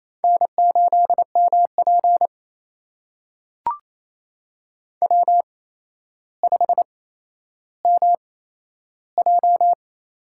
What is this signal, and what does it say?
Morse code, heard as a keyed tone, about 27 wpm, spaced as for 12 wpm, 700 Hz. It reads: N8MP W5MJ
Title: USA Call Signs - 4 Characters - 8x Character Spacing - 28wpm